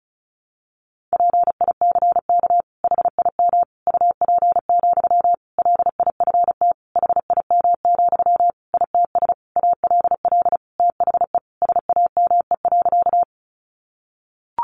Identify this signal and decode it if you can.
Morse code, heard as a keyed tone, about 34 wpm, 700 Hz. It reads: PICK HIM UP, LIFT HIM, ITS ALL THE SAME.
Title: Sentences from Top 1000 Words 35wpm